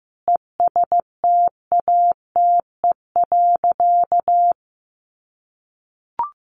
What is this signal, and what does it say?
Morse code, heard as a keyed tone, about 15 wpm, 700 Hz. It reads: ESTATE.
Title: Sentences from Top 1200 Words 15wpm